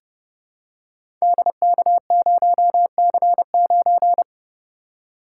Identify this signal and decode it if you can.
Morse code, heard as a keyed tone, about 30 wpm, 700 Hz. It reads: DK0C9
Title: Code Groups - 5 Letter:Numbers - 30wpm